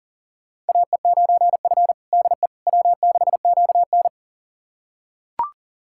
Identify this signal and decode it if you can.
Morse code, heard as a keyed tone, about 40 wpm, 700 Hz. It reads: AE9F DE W6QN
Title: USA Callsign DE Callsign - 4 Characters - 40wpm